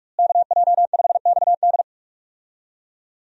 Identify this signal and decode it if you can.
Morse code, heard as a keyed tone, about 45 wpm, 700 Hz. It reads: KJ5XD
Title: USA Call Signs - 5 Characters - Repeated 2x - 45wpm